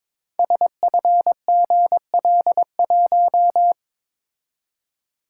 Morse code, heard as a keyed tone, about 22 wpm, 700 Hz. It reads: SFGL1
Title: Code Groups - 5 Letter:Numbers - 22wpm